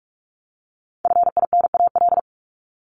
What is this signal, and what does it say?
Morse code, heard as a keyed tone, about 44 wpm, 700 Hz. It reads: FINAL
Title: Top 500 Words Repeated 45wpm